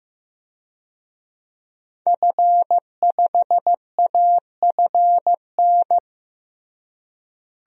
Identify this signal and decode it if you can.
Morse code, heard as a keyed tone, about 15 wpm, 700 Hz. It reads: F5AFN
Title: DXCC-100 Call Signs - 5 Characters - Repeated 3x - 15wpm